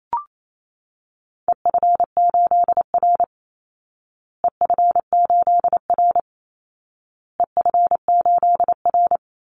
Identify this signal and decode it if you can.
Morse code, heard as a keyed tone, about 28 wpm, 700 Hz. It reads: EF8R EF8R EF8R
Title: Call Signs Repeated 2x - 28wpm